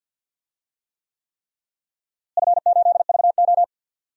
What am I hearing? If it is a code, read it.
Morse code, heard as a keyed tone, about 50 wpm, 700 Hz. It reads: F8VO
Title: DXCC-100 Call Signs - 4 Characters - Repeated 3x - 50wpm